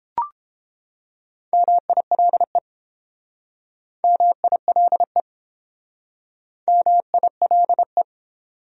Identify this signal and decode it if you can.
Morse code, heard as a keyed tone, about 29 wpm, 700 Hz. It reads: MILE MILE MILE